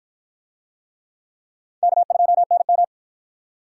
Morse code, heard as a keyed tone, about 53 wpm, 700 Hz. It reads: K2NM